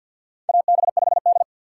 Morse code, heard as a keyed tone, about 53 wpm, 700 Hz. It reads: AB5D
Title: USA Call Signs - 4 Characters - 3x Character Spacing - 50wpm